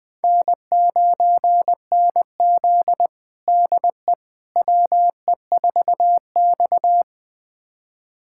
Morse code, heard as a keyed tone, about 20 wpm, 700 Hz. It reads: N9NZ DE WE4X
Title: USA Callsign DE Callsign - 4 Characters - 20wpm